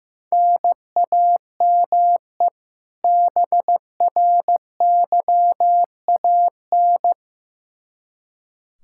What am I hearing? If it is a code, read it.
Morse code, heard as a keyed tone, about 16 wpm, 700 Hz. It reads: NAME BRYAN
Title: K1USN SST QSO Practice 15wpm